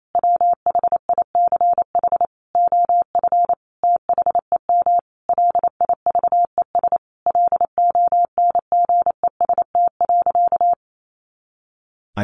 Morse code, heard as a keyed tone, about 28 wpm, 700 Hz. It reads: WHICH OF THEM LIVES LONGEST.
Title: A Sound Of Thunder 28wpm